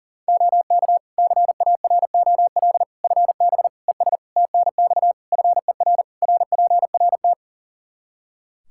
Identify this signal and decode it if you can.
Morse code, heard as a keyed tone, about 40 wpm, 700 Hz. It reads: OK CAROL FB ES TNX FER RPRT
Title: K1USN SST QSO Practice 40wpm